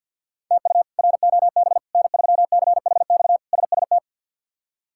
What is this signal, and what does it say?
Morse code, heard as a keyed tone, about 50 wpm, 700 Hz. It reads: TU ROB N3CHX SST
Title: K1USN SST QSO Practice - 50wpm